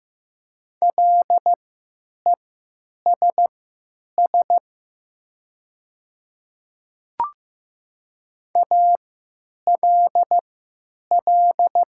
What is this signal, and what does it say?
Morse code, heard as a keyed tone, about 15 wpm, spaced as for 9 wpm, 700 Hz. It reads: LESS ALL
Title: 3x Character Spacing - Top 500 Words - 15wpm